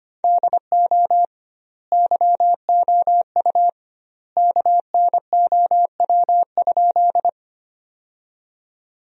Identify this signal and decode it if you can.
Morse code, heard as a keyed tone, about 25 wpm, 700 Hz. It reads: DO YOU KNOW?
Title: 2x Word-Spacing - Sentences from Top 100 Words - 25wpm